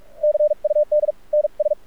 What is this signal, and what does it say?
Morse code, heard as a keyed tone, about 46 wpm, spaced as for 39 wpm, 600 Hz. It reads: CUD NS
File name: archived / CRSnet